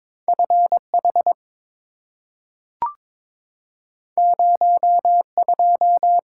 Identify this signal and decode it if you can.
Morse code, heard as a keyed tone, about 22 wpm, 690 Hz. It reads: FH 02
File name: practice / 22wpm / Two Letter-Number Combinations - 22wpm